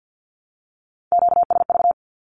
Morse code, heard as a keyed tone, about 50 wpm, 700 Hz. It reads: QSV
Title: Q-Codes 50wpm